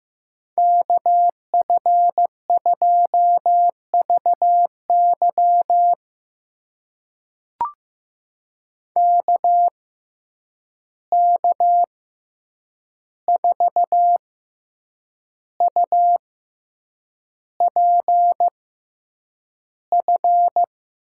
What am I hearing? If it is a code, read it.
Morse code, heard as a keyed tone, about 15 wpm, spaced as for 7 wpm, 700 Hz. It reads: KF2VY KK4UPF
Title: USA Call Signs - 6x Character Spacing - 15wpm